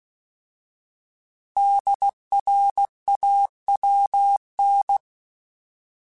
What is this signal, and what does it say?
Morse code, heard as a keyed tone, about 16 wpm, 800 Hz. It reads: DRAWN